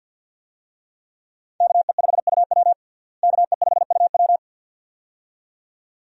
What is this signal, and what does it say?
Morse code, heard as a keyed tone, about 52 wpm, 700 Hz. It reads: KE5UW KE5UW
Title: USA Call Signs - 5 Characters - Repeated 2x - 50wpm